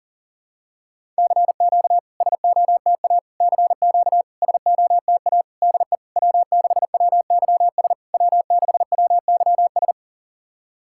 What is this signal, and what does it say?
Morse code, heard as a keyed tone, about 39 wpm, 690 Hz. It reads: CQ SOTA CQ SOTA DE W6WYS W6WYS
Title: SOTA QSO Practice 40wpm